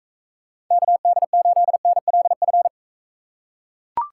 Morse code, heard as a keyed tone, about 42 wpm, 700 Hz. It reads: KD8NLF